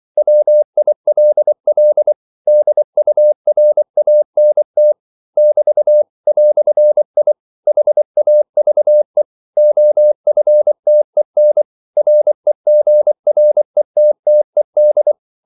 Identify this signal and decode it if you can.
Morse code, heard as a keyed tone, about 24 wpm, 600 Hz. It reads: WILL DURANT <BT> "I HAVE OFTEN REGRETTED
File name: CW / Practice / EnglishQuotations0224